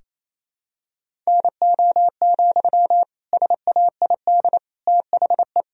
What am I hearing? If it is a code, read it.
Morse code, heard as a keyed tone, about 28 wpm, 700 Hz. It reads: NO, SAID THE
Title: Sentences from Top 1000 Words - Review Part 4 - 28wpm